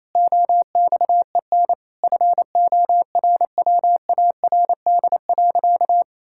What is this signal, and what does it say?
Morse code, heard as a keyed tone, about 28 wpm, 700 Hz. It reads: OXEN FORWARD.
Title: Pangram Challenge v2 - 28wpm